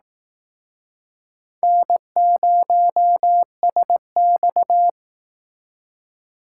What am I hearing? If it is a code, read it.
Morse code, heard as a keyed tone, about 19 wpm, 700 Hz. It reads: N0SX
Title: Speed Racing - USA 1x2 Call Signs 15wpm